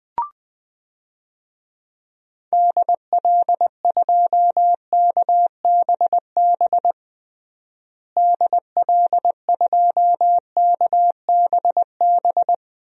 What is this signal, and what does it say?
Morse code, heard as a keyed tone, about 20 wpm, 700 Hz. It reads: DL2KBB DL2KBB
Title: DXCC-100 Call Signs - 6 Characters - Repeated 3x - 20wpm